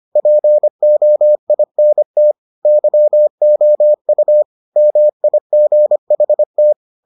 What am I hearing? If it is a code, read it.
Morse code, heard as a keyed tone, about 25 wpm, 600 Hz. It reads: POINT YOU MIGHT